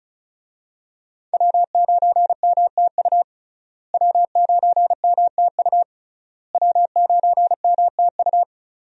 Morse code, heard as a keyed tone, about 36 wpm, 700 Hz. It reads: W9MTU W9MTU W9MTU
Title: USA Call Signs - 5 Characters - Repeated 3x - 35wpm